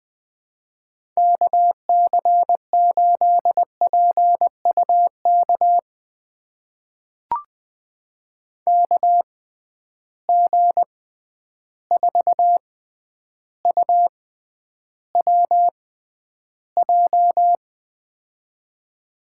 Morse code, heard as a keyed tone, about 20 wpm, spaced as for 10 wpm, 700 Hz. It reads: KC8PUK KG4UWJ
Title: USA Call Signs - 6 Characters - 6x Character Spacing - 20wpm